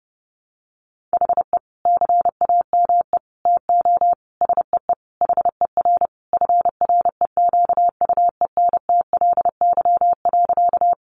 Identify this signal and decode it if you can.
Morse code, heard as a keyed tone, about 30 wpm, 700 Hz. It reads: HE CAME TO SEE HER FREQUENTLY.
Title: Sentences from Top 1300 Words 30wpm